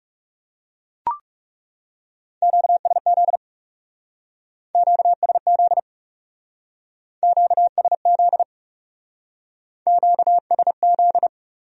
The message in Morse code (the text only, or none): QSZ QSZ QSZ QSZ